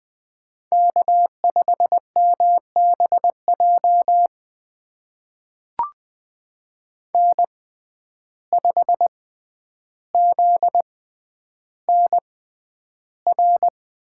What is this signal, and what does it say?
Morse code, heard as a keyed tone, about 21 wpm, spaced as for 10 wpm, 700 Hz. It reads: K5MBJ N5ZNR
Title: USA Call Signs - 5 Characters - 6x Character Spacing - 20wpm